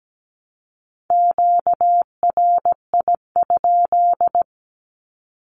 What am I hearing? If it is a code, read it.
Morse code, heard as a keyed tone, about 17 wpm, 700 Hz. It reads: QRI?
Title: Q-Codes Repeated 3x 17wpm